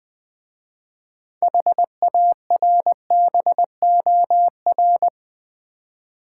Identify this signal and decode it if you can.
Morse code, heard as a keyed tone, about 21 wpm, 700 Hz. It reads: HARBOR
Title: 6 Letter Words - First 3 Letters Given - 20wpm